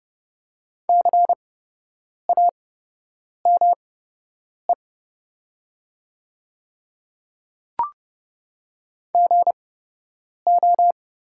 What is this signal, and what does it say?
Morse code, heard as a keyed tone, about 30 wpm, spaced as for 8 wpm, 700 Hz. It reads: CAME GO
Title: 8x Character Spacing - Top 200 Words - 30wpm